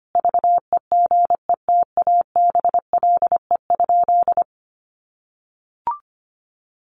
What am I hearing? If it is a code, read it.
Morse code, heard as a keyed tone, about 25 wpm, 700 Hz. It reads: VEGETABLE?
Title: Sentences from Top 1300 Words 25wpm